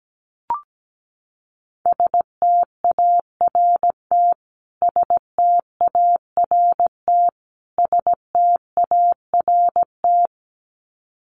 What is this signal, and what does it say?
Morse code, heard as a keyed tone, about 17 wpm, 700 Hz. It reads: START START START
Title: Top 300 Words Repeated 17wpm